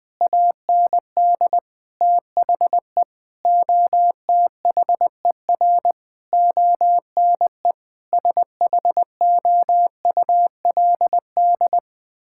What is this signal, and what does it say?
Morse code, heard as a keyed tone, about 20 wpm, 700 Hz. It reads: AND THE OTHER ONE SHOULD
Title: Sentences from Top 1600 Words 20wpm